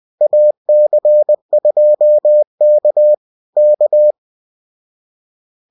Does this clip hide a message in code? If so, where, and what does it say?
Yes: Morse code from 0.2 to 4.1 s, heard as a keyed tone, about 20 wpm, 600 Hz. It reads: AC2K K